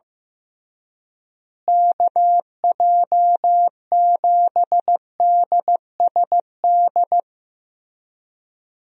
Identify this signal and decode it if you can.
Morse code, heard as a keyed tone, about 15 wpm, 700 Hz. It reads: KJ7DSD
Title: USA Call Signs - 6 Characters - Repeated 3x - 15wpm